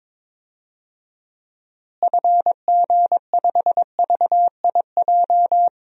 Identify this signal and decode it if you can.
Morse code, heard as a keyed tone, about 22 wpm, 700 Hz. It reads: FG5VIJ
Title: DXCC-100 Call Signs - 6 Characters - Repeated 2x - 22wpm